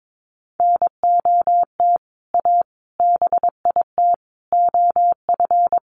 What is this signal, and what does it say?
Morse code, heard as a keyed tone, about 22 wpm, 700 Hz. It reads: NOT A BIT OF
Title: Sentences from Top 1400 Words 22wpm